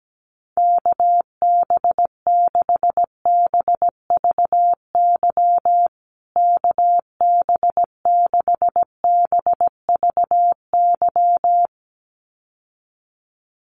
Morse code, heard as a keyed tone, about 17 wpm, 700 Hz. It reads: KB6BVY KB6BVY
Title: USA Call Signs Repeated 2x 17wpm